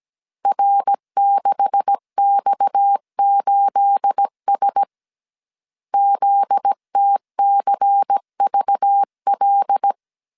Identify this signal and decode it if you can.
Morse code, heard as a keyed tone, about 17 wpm, 800 Hz. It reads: R6X8S ZTCVL
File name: CW / DET / 1 9 1500C S A 16wpm 2828